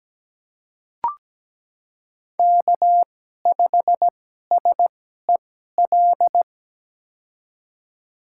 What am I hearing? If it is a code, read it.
Morse code, heard as a keyed tone, about 17 wpm, spaced as for 12 wpm, 700 Hz. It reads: K5SEL
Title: USA Call Signs - 5 Characters - 2x Character Spacing - 17wpm